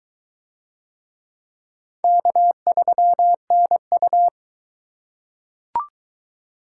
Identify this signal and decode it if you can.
Morse code, heard as a keyed tone, about 22 wpm, 700 Hz. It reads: K3NU